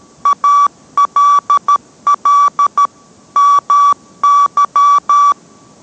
Morse code, heard as a keyed tone, about 13 wpm, 1150 Hz. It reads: ALL MY